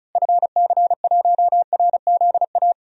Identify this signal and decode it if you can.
Morse code, heard as a keyed tone, about 36 wpm, 700 Hz. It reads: FC1RZA
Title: DXCC-100 Call Signs - 6 Characters - Repeated 2x - 35wpm